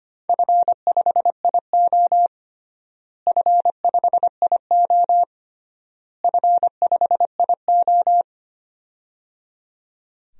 Morse code, heard as a keyed tone, about 25 wpm, 700 Hz. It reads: F5IO F5IO F5IO
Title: DXCC-100 Call Signs - 4 Characters - Repeated 3x - 25wpm